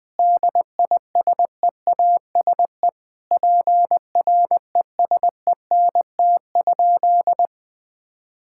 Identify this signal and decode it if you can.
Morse code, heard as a keyed tone, about 21 wpm, 700 Hz. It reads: DISEASE PRESENT?
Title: Sentences from Top 1000 Words - Review Part 6 - 20wpm